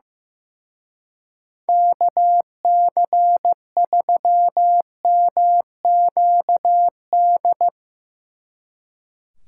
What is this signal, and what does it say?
Morse code, heard as a keyed tone, about 15 wpm, 700 Hz. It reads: KC3MQD
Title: Speed Racing - USA Call Signs 15wpm